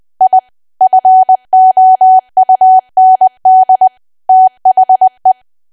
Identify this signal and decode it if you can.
Morse code, heard as a keyed tone, about 20 wpm, 750 Hz. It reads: I FOUND THE
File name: Archive / 20 WPM / 20WPM